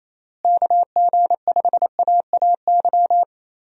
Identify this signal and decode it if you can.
Morse code, heard as a keyed tone, about 28 wpm, 700 Hz. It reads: KG5AAY
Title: USA Call Signs - 6 Characters - Repeated 3x - 28wpm